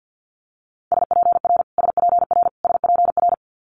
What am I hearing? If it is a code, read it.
Morse code, heard as a keyed tone, about 50 wpm, 700 Hz. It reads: SPR SPR SPR
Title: Multi-letter Phonemes Repeated 3x 50wpm